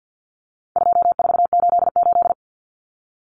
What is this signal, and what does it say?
Morse code, heard as a keyed tone, about 50 wpm, 700 Hz. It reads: 2488
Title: Common Cut Numbers - 4-Digit 50wpm